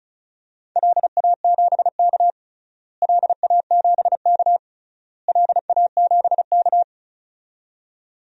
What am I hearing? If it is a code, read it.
Morse code, heard as a keyed tone, about 34 wpm, 700 Hz. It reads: LA7K LA7K LA7K
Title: DXCC-100 Call Signs - 4 Characters - Repeated 3x - 35wpm